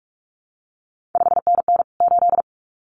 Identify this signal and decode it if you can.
Morse code, heard as a keyed tone, about 46 wpm, 700 Hz. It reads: 5NN 8
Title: CQ WW DX Contest Exchange - 45wpm